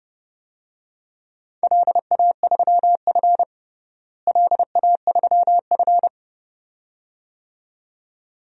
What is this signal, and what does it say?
Morse code, heard as a keyed tone, about 32 wpm, 700 Hz. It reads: LA3F LA3F
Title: DXCC-100 Call Signs - 4 Characters - Repeated 2x - 30wpm